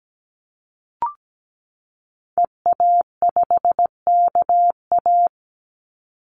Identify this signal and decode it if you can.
Morse code, heard as a keyed tone, about 17 wpm, 700 Hz. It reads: EA5KA
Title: Call Signs Repeated 2x - 17wpm